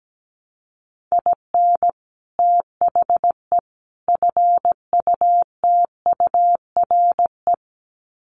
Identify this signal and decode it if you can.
Morse code, heard as a keyed tone, about 17 wpm, 700 Hz. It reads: IN THE FUTURE
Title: Sets of 3 Words 17wpm